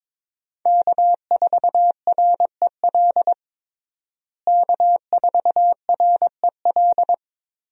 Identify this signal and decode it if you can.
Morse code, heard as a keyed tone, about 21 wpm, 700 Hz. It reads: K4REL K4REL
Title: USA Call Signs - 5 Characters - Repeated 3x - 22wpm